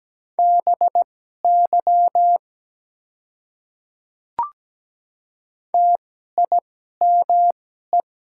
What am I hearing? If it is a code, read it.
Morse code, heard as a keyed tone, about 17 wpm, spaced as for 12 wpm, 700 Hz. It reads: BY TIME
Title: 2x Character Spacing - Top 100 Words - 17wpm